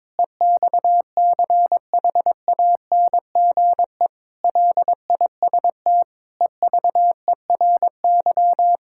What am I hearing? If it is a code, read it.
Morse code, heard as a keyed tone, about 22 wpm, 700 Hz. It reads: EXCHANGE LIST EVERY
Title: Sentences from Top 1000 Words - Review Part 6 - 22wpm